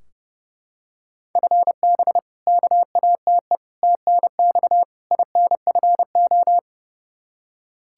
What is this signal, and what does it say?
Morse code, heard as a keyed tone, about 30 wpm, 700 Hz. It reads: FB KATE TNX INFO
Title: Practice QSO - Exchange 3 - More Info - 30wpm